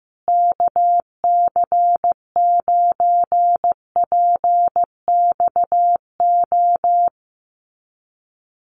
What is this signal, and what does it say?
Morse code, heard as a keyed tone, about 15 wpm, 700 Hz. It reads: KC9PXO
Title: USA Call Signs Repeated 3x 15wpm